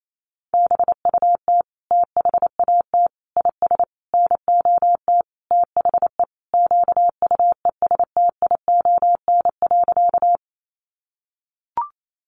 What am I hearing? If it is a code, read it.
Morse code, heard as a keyed tone, about 28 wpm, 700 Hz. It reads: BUT THAT IS NOT THE QUESTION.
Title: Sentences from Top 600 Words 28wpm